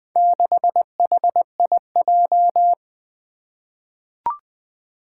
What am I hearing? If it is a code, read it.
Morse code, heard as a keyed tone, about 20 wpm, 700 Hz. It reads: 6HIJ
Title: Four Challenging Letter-Number Combinations 20wpm